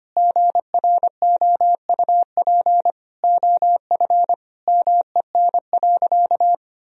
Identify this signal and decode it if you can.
Morse code, heard as a keyed tone, about 25 wpm, 700 Hz. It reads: GROUP OF MEN.
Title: Sentences from Top 1000 Words - Review Part 1 - 25wpm